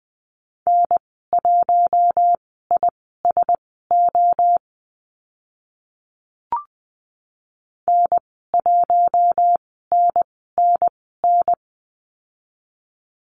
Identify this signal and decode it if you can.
Morse code, heard as a keyed tone, about 20 wpm, spaced as for 14 wpm, 700 Hz. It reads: N1ISO N1NNN
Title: 042 Lesson 8g - Number 1 - Call Signs 20wpm